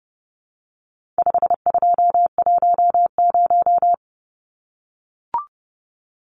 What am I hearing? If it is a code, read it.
Morse code, heard as a keyed tone, about 30 wpm, 700 Hz. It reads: 5210
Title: Numbers - 4-Digit 30wpm